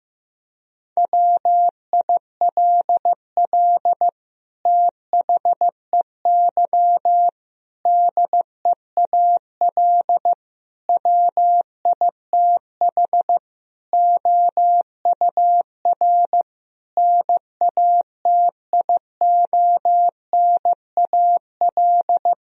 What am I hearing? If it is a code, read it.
Morse code, heard as a keyed tone, about 15 wpm, 700 Hz. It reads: WILL THEY DEAL WITH OUR NATIONAL
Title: Sentences from Top 1000 Words - Review Part 4 - 15wpm